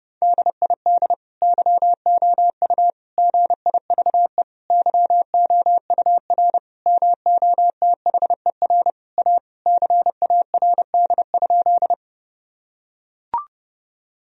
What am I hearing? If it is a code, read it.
Morse code, heard as a keyed tone, about 32 wpm, 700 Hz. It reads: DID YOU GIVE YOUR MOTHER A CARD?
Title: Sentences from Top 1000 Words - Review Part 4 - 30wpm